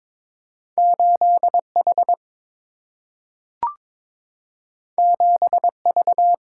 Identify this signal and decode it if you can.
Morse code, heard as a keyed tone, about 22 wpm, 700 Hz. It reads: 8 H 7 V
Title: Single-Letter Single-Number 22wpm